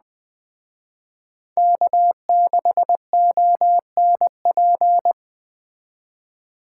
Morse code, heard as a keyed tone, about 20 wpm, 700 Hz. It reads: K6ONP
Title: Speed Racing - USA 1x3 Call Signs 17wpm